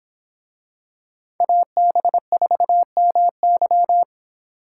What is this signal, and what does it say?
Morse code, heard as a keyed tone, about 25 wpm, 700 Hz. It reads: AB4MY